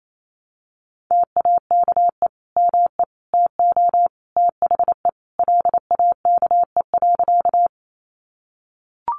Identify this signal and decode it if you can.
Morse code, heard as a keyed tone, about 28 wpm, 700 Hz. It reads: TAKE ME TO THE LAKE.
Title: Sentences from Top 1100 Words 28wpm